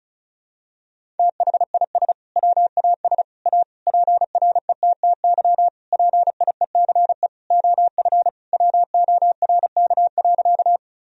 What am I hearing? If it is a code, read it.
Morse code, heard as a keyed tone, about 34 wpm, 700 Hz. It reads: THIS WAS A PRETTY PIECE OF WORK.
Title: Sentences from Top 1000 Words - Review Part 2 - 35wpm